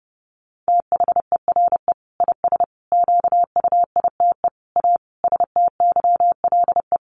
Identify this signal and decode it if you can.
Morse code, heard as a keyed tone, about 30 wpm, 700 Hz. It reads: THERE IS QUITE A STYLE
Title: Sentences from Top 800 Words 30wpm